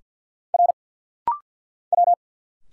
Morse code, heard as a keyed tone, about 50 wpm, 700 Hz. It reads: R W
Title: Single Letter-Number Alt - Rapid-Fire 50wpm